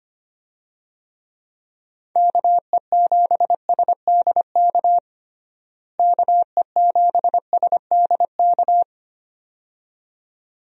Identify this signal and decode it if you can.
Morse code, heard as a keyed tone, about 25 wpm, 700 Hz. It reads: KE7SDK KE7SDK
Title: USA Call Signs - 6 Characters - Repeated 2x - 25wpm